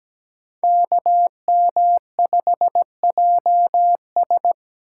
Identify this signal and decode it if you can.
Morse code, heard as a keyed tone, about 20 wpm, 700 Hz. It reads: KM5JS
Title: Speed Racing - USA Call Signs 17wpm